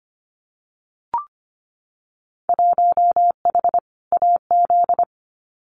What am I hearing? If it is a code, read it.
Morse code, heard as a keyed tone, about 25 wpm, 700 Hz. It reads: 1H AZ
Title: Winter Field Day Exchanges 25wpm